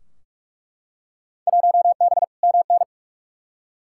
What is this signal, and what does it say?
Morse code, heard as a keyed tone, about 46 wpm, 700 Hz. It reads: 1B MN
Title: Field Day Exchanges - 45wpm